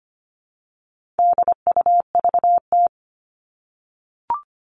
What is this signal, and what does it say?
Morse code, heard as a keyed tone, about 25 wpm, 700 Hz. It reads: DUVT
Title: All Cut Numbers - 4-Digit 25wpm